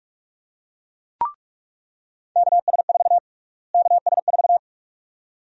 Morse code, heard as a keyed tone, about 41 wpm, 700 Hz. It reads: KS4 KS4
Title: USA Call Sign Prefixes - Repeated 2x - 45wpm